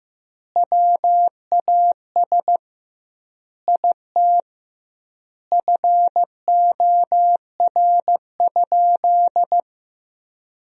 Morse code, heard as a keyed tone, about 15 wpm, 700 Hz. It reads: WAS IT FOR?
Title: 2x Word-Spacing - Sentences from Top 100 Words - 15wpm